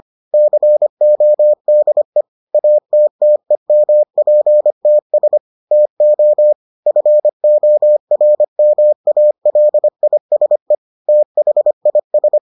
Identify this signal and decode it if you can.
Morse code, heard as a keyed tone, about 25 wpm, 600 Hz. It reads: CODE ATTEMPTS TO FORMALISE THIS